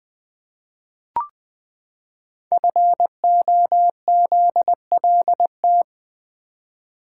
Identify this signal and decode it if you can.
Morse code, heard as a keyed tone, about 20 wpm, 700 Hz. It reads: FOZLT
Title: Code Groups - 5 Letters - 20wpm